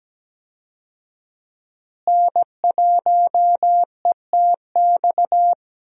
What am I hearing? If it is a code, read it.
Morse code, heard as a keyed tone, about 17 wpm, 700 Hz. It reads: N1ETX
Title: USA Call Signs - 5 Characters - Repeated 3x - 17wpm